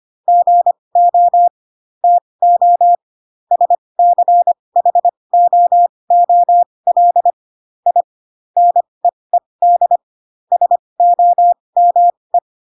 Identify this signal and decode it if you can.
Morse code, heard as a keyed tone, about 25 wpm, spaced as for 20 wpm, 700 Hz. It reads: GO TO SCHOOL I NEED SOME
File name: Pinocchio / Pinocchio ch8 15 25 ik1hge